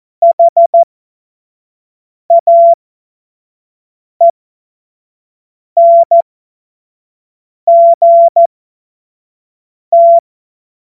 Morse code, heard as a keyed tone, about 14 wpm, spaced as for 5 wpm, 680 Hz. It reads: HAENGT
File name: Morsekurs / Witze / Witz 11. Juli 2023 25 F70